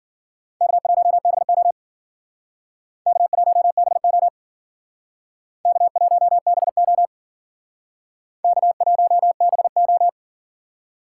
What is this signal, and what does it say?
Morse code, heard as a keyed tone, about 53 wpm, 700 Hz. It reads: K1BO K1BO K1BO K1BO